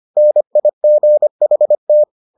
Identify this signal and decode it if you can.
Morse code, heard as a keyed tone, about 25 wpm, 600 Hz. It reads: NIGHT